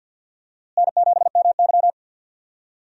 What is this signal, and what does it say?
Morse code, heard as a keyed tone, about 46 wpm, 700 Hz. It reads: N7MY
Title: USA Call Signs - 4 Characters - Repeated 2x - 50wpm